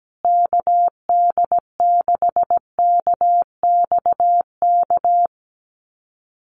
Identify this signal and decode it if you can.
Morse code, heard as a keyed tone, about 17 wpm, 700 Hz. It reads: KD6KXK
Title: USA Call Signs Repeated 2x 17wpm